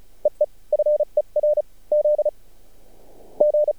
Morse code, heard as a keyed tone, about 34 wpm, 600 Hz. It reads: EE FER Z G
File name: archived / CRSnet